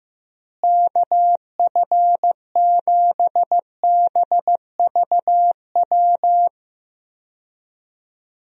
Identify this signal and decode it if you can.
Morse code, heard as a keyed tone, about 15 wpm, 700 Hz. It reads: KF7BVW
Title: USA Call Signs - 6 Characters - Repeated 3x - 15wpm